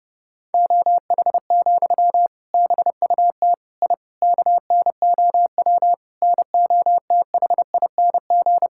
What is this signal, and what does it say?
Morse code, heard as a keyed tone, about 28 wpm, 700 Hz. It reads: OH, BUT I KNOW NOTHING
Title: Sentences from Top 1000 Words - Review Part 6 - 30wpm